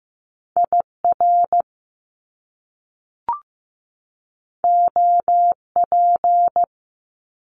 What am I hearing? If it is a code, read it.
Morse code, heard as a keyed tone, about 15 wpm, 700 Hz. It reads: IR OP
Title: Top Two Letter Combinations in Words 15wpm